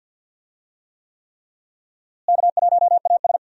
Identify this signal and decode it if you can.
Morse code, heard as a keyed tone, about 47 wpm, 700 Hz. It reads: K1AS
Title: USA Call Signs - 4 Characters - Repeated 3x - 50wpm